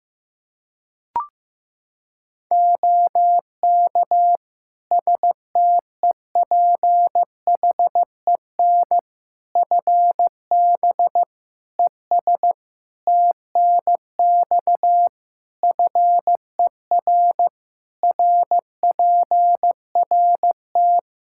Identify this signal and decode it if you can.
Morse code, heard as a keyed tone, about 16 wpm, 700 Hz. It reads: OK STEPHEN FB ES TNX FER RPRT
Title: Practice QSO - Exchange 2 - Info - 15wpm